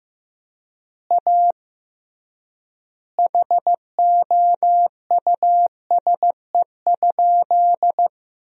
Morse code, heard as a keyed tone, about 15 wpm, 700 Hz. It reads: A HOUSE?
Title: 3x Word-Spacing - Sentences from Top 200 Words - 15wpm